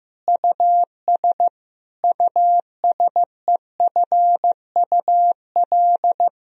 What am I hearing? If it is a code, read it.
Morse code, heard as a keyed tone, about 15 wpm, 700 Hz. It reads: US USEFUL
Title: Sentences from Top 1000 Words - Review Part 4 - 15wpm